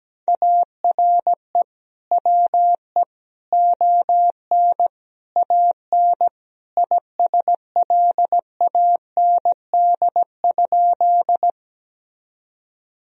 Morse code, heard as a keyed tone, about 17 wpm, 700 Hz. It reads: ARE WE ON AN ISLAND?
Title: Sentences from Top 1000 Words - Review Part 3 - 17wpm